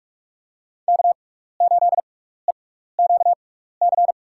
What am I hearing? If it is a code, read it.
Morse code, heard as a keyed tone, about 45 wpm, spaced as for 15 wpm, 700 Hz. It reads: K8EQC
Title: USA Call Signs - 5 Characters - 6x Character Spacing - 45wpm